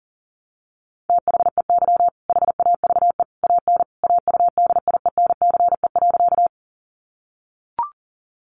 Morse code, heard as a keyed tone, about 40 wpm, 700 Hz. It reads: THEY HAVE AN AUDIENCE.
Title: Sentences from Top 800 Words 40wpm